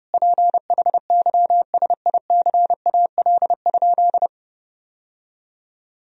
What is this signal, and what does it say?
Morse code, heard as a keyed tone, about 30 wpm, 700 Hz. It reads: PHYSICAL?
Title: Sentences from Top 1000 Words - Review Part 4 - 30wpm